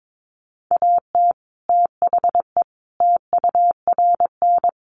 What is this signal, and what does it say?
Morse code, heard as a keyed tone, about 22 wpm, 700 Hz. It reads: AT THE TURN
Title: Sets of 5 Words 22wpm